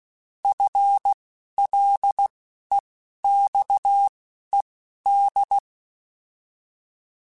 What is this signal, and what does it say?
Morse code, heard as a keyed tone, about 16 wpm, spaced as for 12 wpm, 800 Hz. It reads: FLEXED
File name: FL / FL mp3s / Fast FL @10